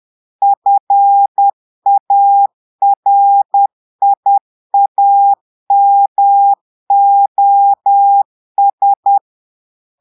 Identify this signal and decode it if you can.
Morse code, heard as a keyed tone, about 10 wpm, 800 Hz. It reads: FARIAMOS
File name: Portugais - texte en clair 1 - vitesse - 10 mots minute